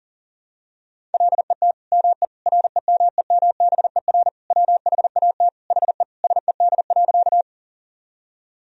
Morse code, heard as a keyed tone, about 40 wpm, 700 Hz. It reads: LET ME REMEMBER WHAT HE SED.
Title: Sentences from Top 400 Words and Abbreviations 40wpm